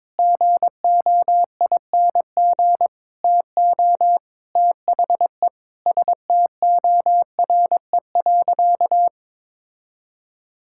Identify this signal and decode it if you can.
Morse code, heard as a keyed tone, about 22 wpm, 700 Hz. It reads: GOING TO THE STORE.
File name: practice / 22wpm / Sentences with 4-Word Phrases - 22wpm